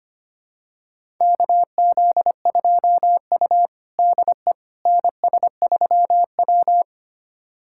Morse code, heard as a keyed tone, about 25 wpm, 700 Hz. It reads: KZ2U DE NS3W
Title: USA Callsign DE Callsign - 4 Characters - 25wpm